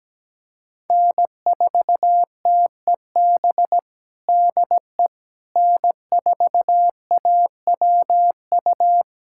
Morse code, heard as a keyed tone, about 17 wpm, 700 Hz. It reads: N4TEB DE N4AWU
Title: USA Callsign DE Callsign - 5 Characters - 17wpm